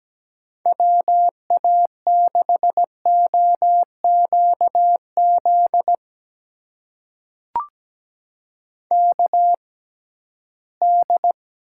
Morse code, heard as a keyed tone, about 17 wpm, spaced as for 13 wpm, 700 Hz. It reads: WA6OQZ KD
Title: USA Call Signs - 6x Character Spacing - 17wpm